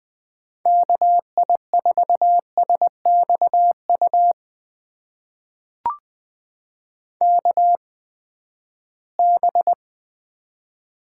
Morse code, heard as a keyed tone, about 20 wpm, spaced as for 14 wpm, 700 Hz. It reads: KI4SXU KB
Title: USA Call Signs - 6 Characters - 8x Character Spacing - 20wpm